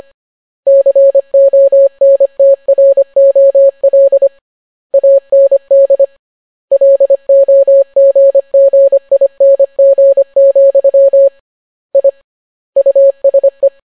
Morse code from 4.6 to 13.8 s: AND LOGGING, I USE